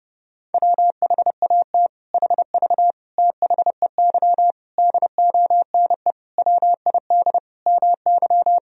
Morse code, heard as a keyed tone, about 30 wpm, 700 Hz. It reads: WHAT HV THEY DONE WID MY
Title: Sentences from Top 500 Words and Abbreviations 30wpm